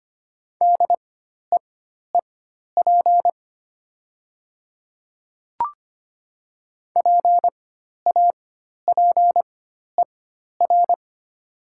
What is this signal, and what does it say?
Morse code, heard as a keyed tone, about 25 wpm, spaced as for 12 wpm, 700 Hz. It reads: DEEP PAPER
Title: 4x Character Spacing - Top 500 Words - 25wpm